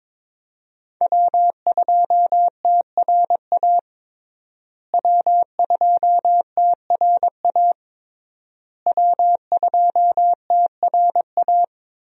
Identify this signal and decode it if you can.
Morse code, heard as a keyed tone, about 22 wpm, 700 Hz. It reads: W2TRA W2TRA W2TRA
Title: USA Call Signs - 5 Characters - Repeated 3x - 22wpm